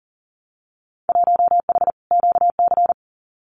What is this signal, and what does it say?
Morse code, heard as a keyed tone, about 40 wpm, 700 Hz. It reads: 1H QC
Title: Winter Field Day Exchanges 40wpm